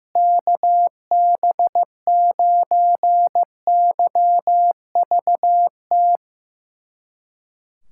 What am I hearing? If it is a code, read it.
Morse code, heard as a keyed tone, about 15 wpm, 700 Hz. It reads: KB9YVT
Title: USA Callsign DE Callsign - 6 Characters - 15wpm